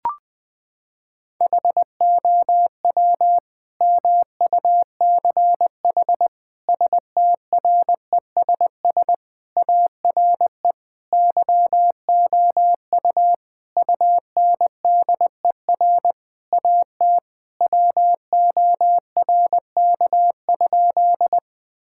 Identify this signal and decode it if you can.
Morse code, heard as a keyed tone, about 20 wpm, 700 Hz. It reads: HOW MUCH STRESS ARE YOU UNDER AT WORK?
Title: Sentences from Top 1000 Words - Review Part 6 - 20wpm